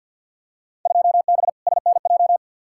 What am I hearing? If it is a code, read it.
Morse code, heard as a keyed tone, about 49 wpm, 700 Hz. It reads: 2B SNJ
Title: Field Day Exchanges Encore - 50wpm